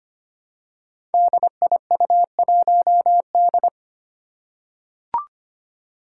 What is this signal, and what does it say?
Morse code, heard as a keyed tone, about 25 wpm, 700 Hz. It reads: DIU1D
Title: Five Challenging Letter-Number Combinations 25wpm